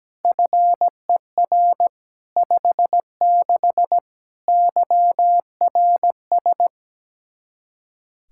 Morse code, heard as a keyed tone, about 17 wpm, 700 Hz. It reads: FER 56 YRS
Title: K1USN SST QSO Practice 17wpm